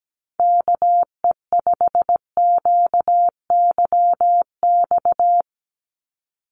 Morse code, heard as a keyed tone, about 17 wpm, 700 Hz. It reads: KE5QYX
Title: USA Call Signs 17wpm